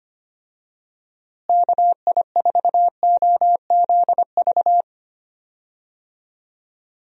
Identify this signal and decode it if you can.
Morse code, heard as a keyed tone, about 25 wpm, 700 Hz. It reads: KI4OZV
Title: USA Call Signs - 6 Characters - Repeated 3x - 25wpm